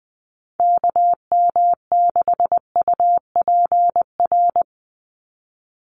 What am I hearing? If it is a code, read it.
Morse code, heard as a keyed tone, about 20 wpm, 700 Hz. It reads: KM6UPR
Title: USA Call Signs 20wpm